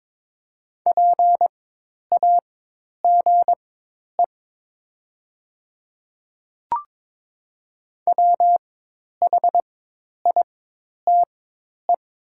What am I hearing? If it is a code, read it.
Morse code, heard as a keyed tone, about 22 wpm, spaced as for 10 wpm, 700 Hz. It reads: PAGE WHITE
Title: 4x Character Spacing - Top 300 Words - 22wpm